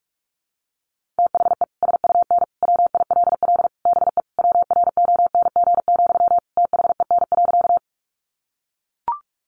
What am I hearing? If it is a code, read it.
Morse code, heard as a keyed tone, about 44 wpm, 700 Hz. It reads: THE SUN WILL BE WRONG, THEN.
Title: Sentences from Top 900 Words 45wpm